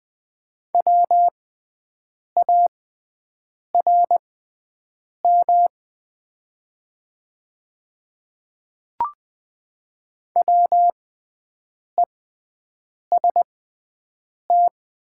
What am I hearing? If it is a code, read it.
Morse code, heard as a keyed tone, about 20 wpm, spaced as for 7 wpm, 700 Hz. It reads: WARM WEST
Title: 6x Character Spacing - Top 500 Words - 20wpm